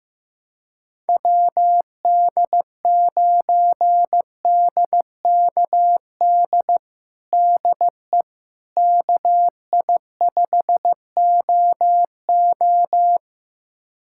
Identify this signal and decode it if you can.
Morse code, heard as a keyed tone, about 15 wpm, 700 Hz. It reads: WD9DKD DE KI5OO
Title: USA Callsign DE Callsign 15wpm